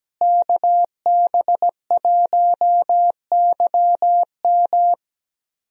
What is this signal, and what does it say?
Morse code, heard as a keyed tone, about 17 wpm, 700 Hz. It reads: KB1YM
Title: USA Call Signs - 5 Characters - Repeated 2x - 17wpm